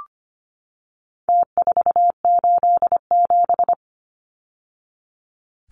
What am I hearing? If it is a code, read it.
Morse code, heard as a keyed tone, about 25 wpm, 700 Hz. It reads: T487
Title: Numbers - Contest Style - 4-Digit 25wpm